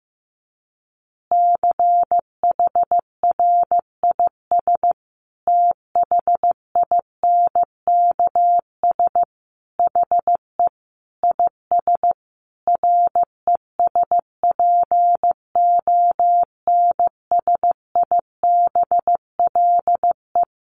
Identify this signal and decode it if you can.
Morse code, heard as a keyed tone, about 15 wpm, 700 Hz. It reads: CHRIS THINKS HE IS RESPONSIBLE
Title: Easy Phrases 15wpm